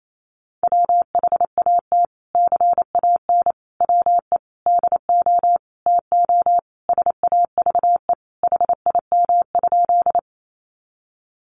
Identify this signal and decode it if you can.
Morse code, heard as a keyed tone, about 28 wpm, 700 Hz. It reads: WHAT CAN WE DO TO SAVE HIM?
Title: Sentences from Top 1100 Words 28wpm